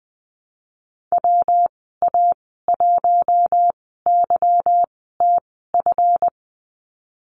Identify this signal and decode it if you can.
Morse code, heard as a keyed tone, about 20 wpm, spaced as for 15 wpm, 700 Hz. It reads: WA1YTF
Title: 186 Lesson 30g - Letter b - Call Signs 20wpm